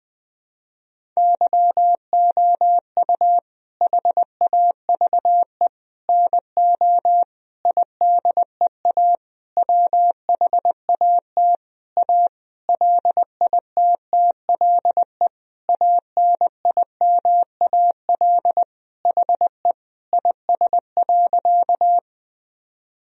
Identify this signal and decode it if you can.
Morse code, heard as a keyed tone, about 21 wpm, 700 Hz. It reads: YOU HAVE NO IDEA WHAT A LITTLE ANIMAL HE IS.
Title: Sentences from Top 1000 Words - Review Part 1 - 20wpm